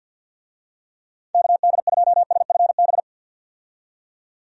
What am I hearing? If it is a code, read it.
Morse code, heard as a keyed tone, about 48 wpm, 700 Hz. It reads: KD2SFB
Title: USA Call Signs - 6 Characters - 50wpm